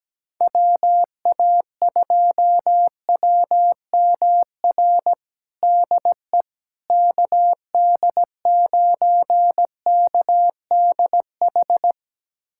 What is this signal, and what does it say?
Morse code, heard as a keyed tone, about 17 wpm, 700 Hz. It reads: WA2WMR DE KD9KDH
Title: USA Callsign DE Callsign - 6 Characters - 17wpm